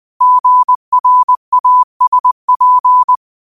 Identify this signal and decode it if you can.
Morse code, heard as a keyed tone, about 20 wpm, 1000 Hz. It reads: GRASP
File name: qotdfiles / 20wpm